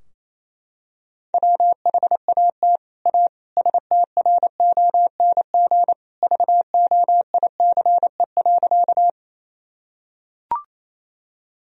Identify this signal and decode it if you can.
Morse code, heard as a keyed tone, about 27 wpm, 700 Hz. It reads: WHAT A STRONG VOICE.
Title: Sentences from Top 1000 Words - Review Part 3 - 28wpm